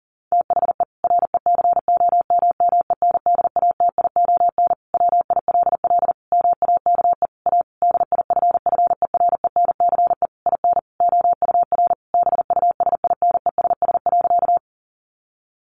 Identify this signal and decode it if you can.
Morse code, heard as a keyed tone, about 40 wpm, 700 Hz. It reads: THE RECOMMENDATION WILL MAKE A DIFFERENCE IN OUR BUSINESS.
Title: Sentences from Top 1400 Words 40wpm